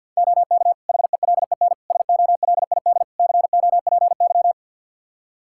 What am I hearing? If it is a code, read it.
Morse code, heard as a keyed tone, about 46 wpm, 700 Hz. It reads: OK HELEN SOLID COPY
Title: Practice QSO - Exchange 3 - More Info - 50wpm